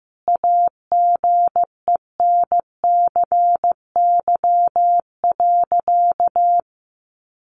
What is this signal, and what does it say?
Morse code, heard as a keyed tone, about 15 wpm, 700 Hz. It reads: AGENCY.
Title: Sentences from Top 1300 Words 15wpm